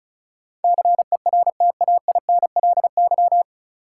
Morse code, heard as a keyed tone, about 36 wpm, 700 Hz. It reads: CERTAINLY
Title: Sentences from Top 900 Words - New Word Drill - 35wpm